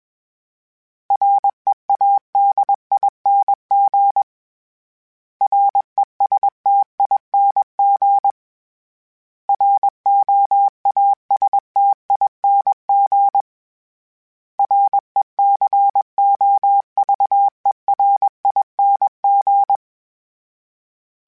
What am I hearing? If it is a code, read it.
Morse code, heard as a keyed tone, about 21 wpm, 800 Hz. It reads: READING RESTING ROASTING RECOVERING
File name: Special INGs / Special INGs-mp3s / Sngl Ing @20